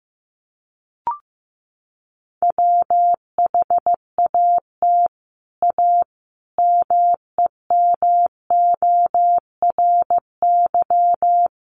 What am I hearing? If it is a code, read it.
Morse code, heard as a keyed tone, about 14 wpm, 700 Hz. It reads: WHAT A MEMORY
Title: Sentences from Top 1300 Words 15wpm